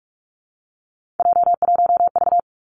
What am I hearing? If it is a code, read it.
Morse code, heard as a keyed tone, about 46 wpm, 700 Hz. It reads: J1V
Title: Three Challenging Letter-Number Combinations 45wpm